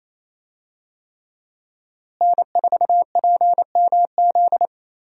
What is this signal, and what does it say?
Morse code, heard as a keyed tone, about 28 wpm, 700 Hz. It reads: N4PMZ
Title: USA Call Signs - 5 Characters - Repeated 3x - 28wpm